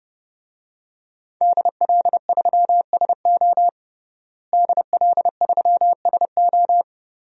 Morse code, heard as a keyed tone, about 30 wpm, 700 Hz. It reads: DL3SO DL3SO
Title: DXCC-100 Call Signs - 5 Characters - Repeated 3x - 30wpm